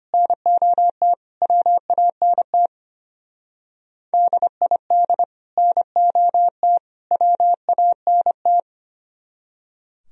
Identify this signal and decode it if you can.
Morse code, heard as a keyed tone, about 32 wpm, 700 Hz. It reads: NOT WANT DID NOT WANT